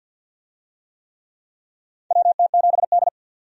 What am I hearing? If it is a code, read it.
Morse code, heard as a keyed tone, about 50 wpm, 700 Hz. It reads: WT7D
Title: USA Call Signs - 4 Characters - 50wpm